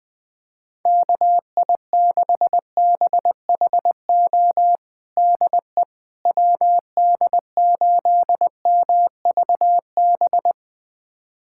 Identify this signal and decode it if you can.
Morse code, heard as a keyed tone, about 21 wpm, 700 Hz. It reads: KI6BHO DE WD8MVB
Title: USA Callsign DE Callsign 20wpm